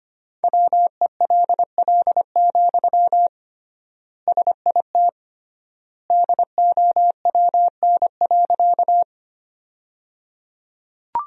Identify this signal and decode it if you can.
Morse code, heard as a keyed tone, about 25 wpm, 700 Hz. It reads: WELL, SIT DOWN.
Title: 3x Word-Spacing - Sentences from Top 500 Words - 25wpm